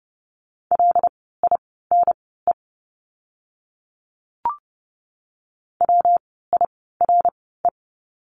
Morse code, heard as a keyed tone, about 30 wpm, spaced as for 17 wpm, 700 Hz. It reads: LINE WIRE
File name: course / 30:10wpm / 115 Lesson 19f - Letter w - Words 30wpm